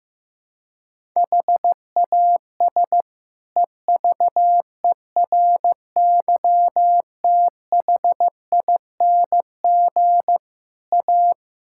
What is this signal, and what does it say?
Morse code, heard as a keyed tone, about 15 wpm, 700 Hz. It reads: HAS EVERYTHING A
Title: Sentences from Top 1600 Words 15wpm